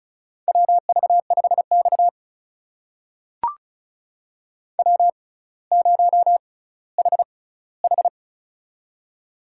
Morse code, heard as a keyed tone, about 36 wpm, spaced as for 18 wpm, 700 Hz. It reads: WV5X W0HH
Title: USA Call Signs - 6x Character Spacing - 35wpm